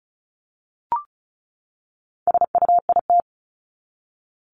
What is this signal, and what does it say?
Morse code, heard as a keyed tone, about 34 wpm, 700 Hz. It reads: SUIT
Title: Top 1501-1750 Words 35wpm